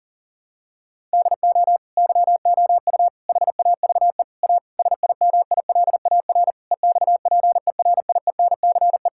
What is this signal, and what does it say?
Morse code, heard as a keyed tone, about 40 wpm, 700 Hz. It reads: DO YOU HAVE A SIMILAR EXPERIENCE
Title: Sentences from Top 1000 Words - Review Part 5 - 40wpm